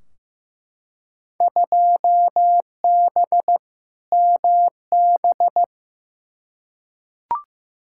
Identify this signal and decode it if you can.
Morse code, heard as a keyed tone, about 15 wpm, 700 Hz. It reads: 2B MB
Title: Field Day Exchanges Encore - 15wpm